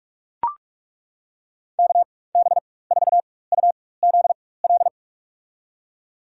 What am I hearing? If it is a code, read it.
Morse code, heard as a keyed tone, about 41 wpm, spaced as for 20 wpm, 700 Hz. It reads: KB4UZL
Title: USA Call Signs - 6 Characters - 4x Character Spacing - 45wpm